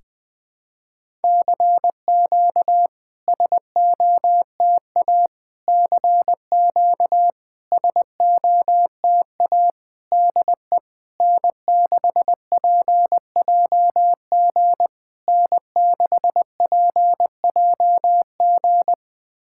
Morse code, heard as a keyed tone, about 20 wpm, 690 Hz. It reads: CQ SOTA CQ SOTA DE N6PJG N6PJG
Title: SOTA QSO Practice 20wpm